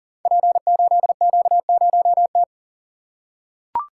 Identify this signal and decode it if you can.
Morse code, heard as a keyed tone, about 39 wpm, 700 Hz. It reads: P8Q0T
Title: Code Groups - 5 Letter:Numbers - 40wpm